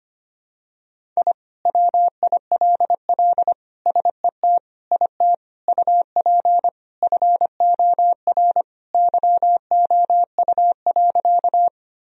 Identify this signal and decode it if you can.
Morse code, heard as a keyed tone, about 25 wpm, 700 Hz. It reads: I WILL SET IT UP FOR YOU.
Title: Sentences from Top 1000 Words - Review Part 1 - 25wpm